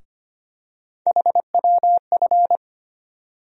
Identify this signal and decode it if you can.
Morse code, heard as a keyed tone, about 25 wpm, 700 Hz. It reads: HWF
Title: USA Call Sign Postfixes - 3 Letters 25wpm